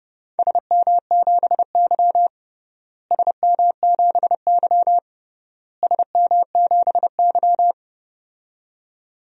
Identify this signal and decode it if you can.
Morse code, heard as a keyed tone, about 32 wpm, 700 Hz. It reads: SM7Y SM7Y SM7Y
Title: DXCC-100 Call Signs - 4 Characters - Repeated 3x - 30wpm